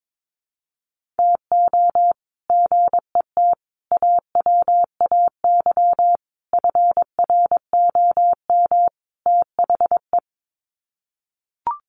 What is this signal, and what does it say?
Morse code, heard as a keyed tone, about 22 wpm, 700 Hz. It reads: TO GET AWAY FROM THE
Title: Sets of 5 Words Encore2 22wpm